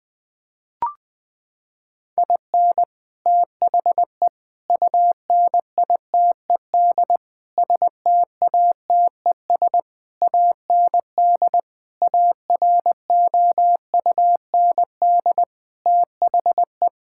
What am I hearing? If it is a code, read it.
Morse code, heard as a keyed tone, about 21 wpm, 700 Hz. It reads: IN THE UNITED STATES AND AROUND THE
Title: Sets of 8 Words - 20wpm